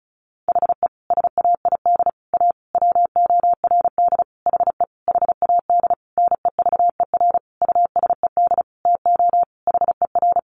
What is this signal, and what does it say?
Morse code, heard as a keyed tone, about 34 wpm, 700 Hz. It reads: HE SAID A WORD HE HAD NEVER USED TO HER
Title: Sentences from Top 600 Words 35wpm